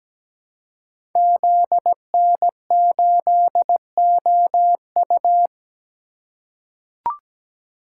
Morse code, heard as a keyed tone, about 17 wpm, 700 Hz. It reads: ZN8OU
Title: Code Groups - 5 Letter:Numbers - 17wpm